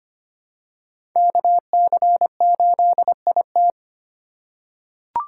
Morse code, heard as a keyed tone, about 25 wpm, 700 Hz. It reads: KC8IT
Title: USA Call Signs - 2x Character Spacing - 25wpm